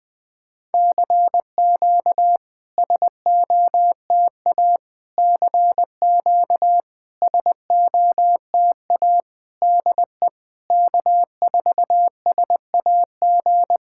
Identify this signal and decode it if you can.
Morse code, heard as a keyed tone, about 20 wpm, 690 Hz. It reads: CQ SOTA CQ SOTA DE K4SAG
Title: SOTA QSO Practice 20wpm